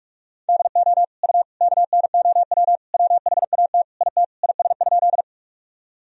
Morse code, heard as a keyed tone, about 46 wpm, 700 Hz. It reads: DO U KNOW WHAT IT IS?
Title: Sentences from Top 100 Words and Abbreviations 45wpm